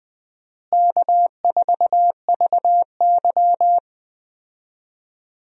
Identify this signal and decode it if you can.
Morse code, heard as a keyed tone, about 20 wpm, 700 Hz. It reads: K4VY
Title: USA Call Signs - 4 Characters - Repeated 2x - 20wpm